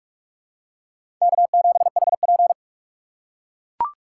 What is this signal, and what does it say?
Morse code, heard as a keyed tone, about 43 wpm, 700 Hz. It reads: K7HP
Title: USA Call Signs - 4 Characters - 4x Character Spacing - 45wpm